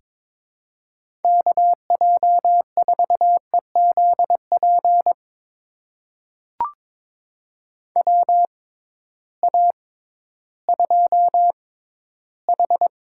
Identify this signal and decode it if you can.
Morse code, heard as a keyed tone, about 22 wpm, spaced as for 13 wpm, 700 Hz. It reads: KJ4EZP WA2H
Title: USA Call Signs - 6 Characters - 6x Character Spacing - 22wpm